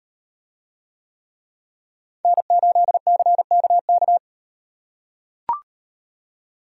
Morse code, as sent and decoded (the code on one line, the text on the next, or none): -. ---.. -.-. -.- -.-
N8CKK